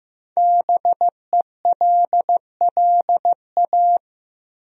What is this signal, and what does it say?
Morse code, heard as a keyed tone, about 15 wpm, 700 Hz. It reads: BELLA
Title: USA Common Names - Max 5 Letters - 15wpm